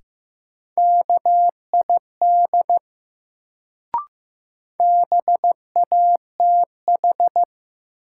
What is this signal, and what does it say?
Morse code, heard as a keyed tone, about 15 wpm, 700 Hz. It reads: KID BATH
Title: Sentences from Top 1600 Words - New Word Drill Alt - 15wpm